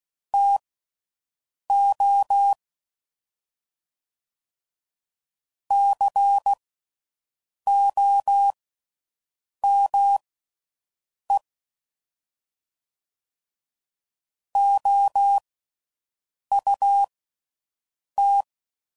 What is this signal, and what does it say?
Morse code, heard as a keyed tone, about 16 wpm, spaced as for 6 wpm, 800 Hz. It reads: TO COME OUT